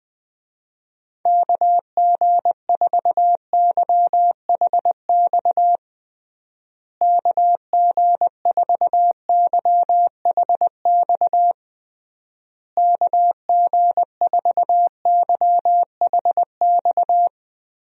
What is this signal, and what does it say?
Morse code, heard as a keyed tone, about 20 wpm, 700 Hz. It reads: KG4YHX KG4YHX KG4YHX
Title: USA Call Signs - 6 Characters - Repeated 3x - 20wpm